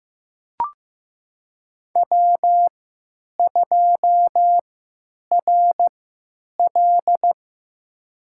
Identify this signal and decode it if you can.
Morse code, heard as a keyed tone, about 15 wpm, spaced as for 9 wpm, 700 Hz. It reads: W2RL
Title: USA Call Signs - 4 Characters - 3x Character Spacing - 15wpm